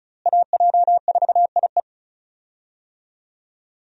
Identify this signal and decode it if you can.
Morse code, heard as a keyed tone, about 36 wpm, 700 Hz. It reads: AJ4IE
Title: USA Call Signs - 5 Characters - Repeated 3x - 35wpm